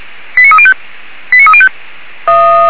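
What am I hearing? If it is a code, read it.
Five-tone selective call (ZVEI), heard as quick runs of five tones, tones of about 70 ms. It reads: 89305 89305
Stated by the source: ZVEI five-tone selective call received by SDR.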